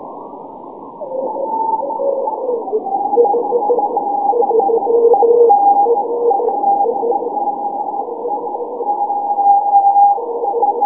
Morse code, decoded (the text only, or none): none